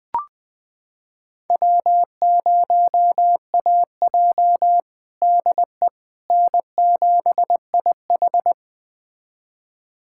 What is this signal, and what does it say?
Morse code, heard as a keyed tone, about 21 wpm, 700 Hz. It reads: W0AJ DE N7IH
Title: USA Callsign DE Callsign - 4 Characters - 20wpm